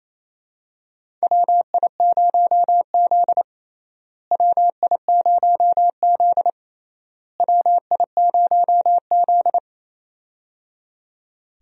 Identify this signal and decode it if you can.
Morse code, heard as a keyed tone, about 28 wpm, 700 Hz. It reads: WI0Z WI0Z WI0Z
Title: USA Call Signs - 4 Characters - Repeated 3x - 28wpm